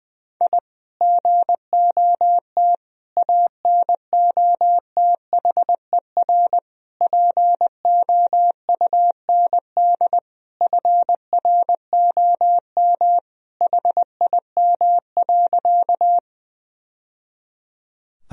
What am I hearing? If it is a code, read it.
Morse code, heard as a keyed tone, about 20 wpm, 700 Hz. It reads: I GOT ANOTHER POUND FROM HIM.
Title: Sentences from Top 1000 Words - Review Part 2 - 20wpm